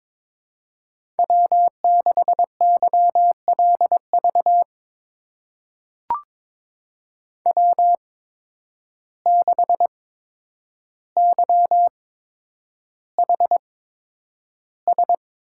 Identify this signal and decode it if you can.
Morse code, heard as a keyed tone, about 22 wpm, spaced as for 9 wpm, 700 Hz. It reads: W6YLV W6YHS
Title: USA Call Signs - 5 Characters - 8x Character Spacing - 22wpm